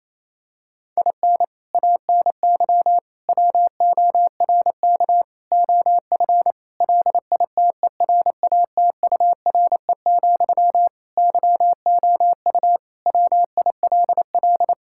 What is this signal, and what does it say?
Morse code, heard as a keyed tone, about 27 wpm, 700 Hz. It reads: IN ANY WORK OF LITERATURE, YOU WILL
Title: Sentences from Top 1000 Words - Review Part 6 - 28wpm